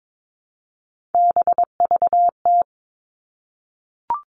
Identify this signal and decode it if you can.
Morse code, heard as a keyed tone, about 22 wpm, 700 Hz. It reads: BVT
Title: All Cut Numbers - 3-Digit 22wpm